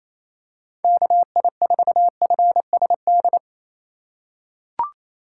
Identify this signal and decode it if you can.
Morse code, heard as a keyed tone, about 28 wpm, 700 Hz. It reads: KI4FSD
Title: USA Call Signs - 6 Characters - 2x Character Spacing - 28wpm